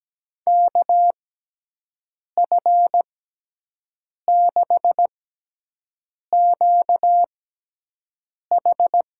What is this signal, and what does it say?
Morse code, heard as a keyed tone, about 17 wpm, spaced as for 6 wpm, 700 Hz. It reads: KF6QH
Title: USA Call Signs - 6x Character Spacing - 17wpm